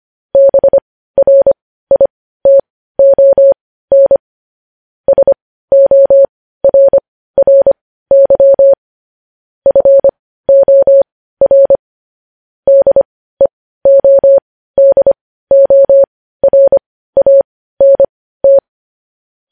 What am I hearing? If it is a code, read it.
Morse code, heard as a keyed tone, about 25 wpm, spaced as for 15 wpm, 550 Hz. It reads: BRITON SORRY FOR DEODORANT
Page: news headlines in morse code at 15 wpm